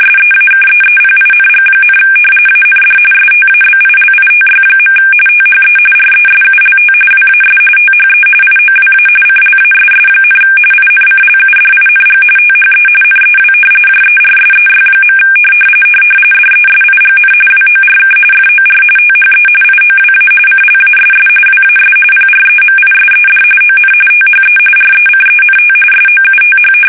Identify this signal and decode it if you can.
RTTY, about 45.45 baud. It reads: SEAKEEPING, AND STABILITY. THE RESULT WAS THE IOWA CLASS, INCLUDING THE NEW JERSEY. DESIGN WORK FOR THE IOWA CLASS WAS PREMISED ON THE RECENT SOUTH
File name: fsk850 / battleshipnj